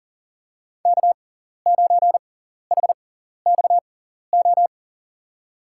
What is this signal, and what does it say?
Morse code, heard as a keyed tone, about 40 wpm, spaced as for 14 wpm, 700 Hz. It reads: K9HXO
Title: USA Call Signs - 6x Character Spacing - 40wpm